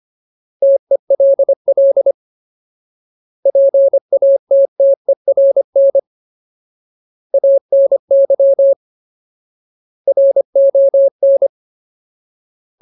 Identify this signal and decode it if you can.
Morse code, heard as a keyed tone, about 25 wpm, 550 Hz. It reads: TELL PATTERN ANY RON
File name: Ultimate Word List / 25WPM Ultimate